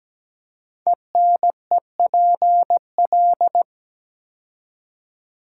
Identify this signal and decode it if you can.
Morse code, heard as a keyed tone, about 17 wpm, 700 Hz. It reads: ENEPL
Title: Code Groups - 5 Letters - 17wpm